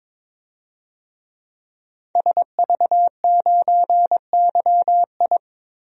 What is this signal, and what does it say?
Morse code, heard as a keyed tone, about 21 wpm, 700 Hz. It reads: SV9YI
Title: DXCC-100 Call Signs - 5 Characters - 22wpm